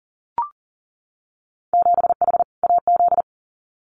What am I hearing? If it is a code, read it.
Morse code, heard as a keyed tone, about 40 wpm, 700 Hz. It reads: 7H AZ
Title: Winter Field Day Exchanges Encore 40wpm